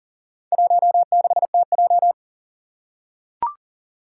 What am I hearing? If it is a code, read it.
Morse code, heard as a keyed tone, about 40 wpm, 700 Hz. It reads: 16TJ
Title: Four Challenging Letter-Number Combinations 40wpm